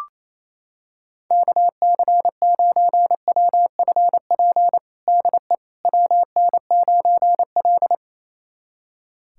Morse code, heard as a keyed tone, about 28 wpm, 700 Hz. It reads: KC9WFP DE WN9L
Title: USA Callsign DE Callsign 28wpm